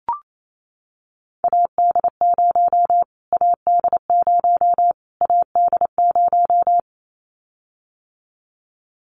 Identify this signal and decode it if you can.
Morse code, heard as a keyed tone, about 28 wpm, 700 Hz. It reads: AD0 AD0 AD0
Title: Callsign Prefixes for Top 25 DXCC - Repeated 3x - 28wpm